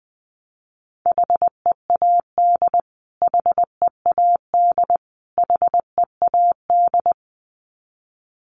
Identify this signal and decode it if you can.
Morse code, heard as a keyed tone, about 20 wpm, 700 Hz. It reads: HEAD HEAD HEAD
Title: Top 300 Words Repeated 20wpm